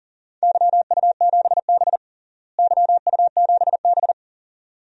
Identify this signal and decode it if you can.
Morse code, heard as a keyed tone, about 40 wpm, 700 Hz. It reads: YU7B YU7B
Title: DXCC-100 Call Signs - 4 Characters - Repeated 2x - 40wpm